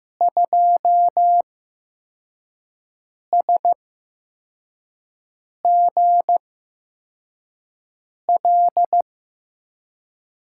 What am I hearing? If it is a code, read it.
Morse code, heard as a keyed tone, about 15 wpm, spaced as for 4 wpm, 700 Hz. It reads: 2SGL
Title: USA Call Signs - 6 Characters - 8x Character Spacing - 15wpm